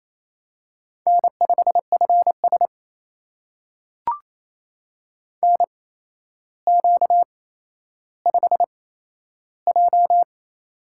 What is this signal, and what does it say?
Morse code, heard as a keyed tone, about 27 wpm, spaced as for 12 wpm, 700 Hz. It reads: N5FS NQ5J
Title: USA Call Signs - 4 Characters - 8x Character Spacing - 28wpm